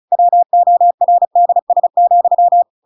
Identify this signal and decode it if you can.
Morse code, heard as a keyed tone, about 34 wpm, 700 Hz. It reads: WORDS,